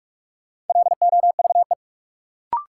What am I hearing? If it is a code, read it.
Morse code, heard as a keyed tone, about 46 wpm, 700 Hz. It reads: LOVE
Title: Sentences from Top 400 Words - New Word Drill - 45wpm